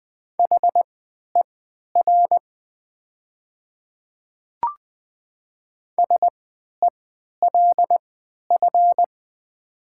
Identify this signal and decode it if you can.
Morse code, heard as a keyed tone, about 20 wpm, spaced as for 11 wpm, 700 Hz. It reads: HER SELF
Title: 3x Character Spacing - Top 200 Words - 20wpm